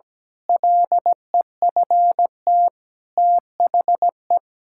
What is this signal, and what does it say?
Morse code, heard as a keyed tone, about 17 wpm, 700 Hz. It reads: LEFT THE
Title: Pangram Challenge v2 - 17wpm